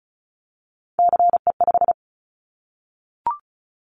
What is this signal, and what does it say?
Morse code, heard as a keyed tone, about 36 wpm, 700 Hz. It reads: CE5
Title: Callsign Prefixes for Top 100 DXCC - 35wpm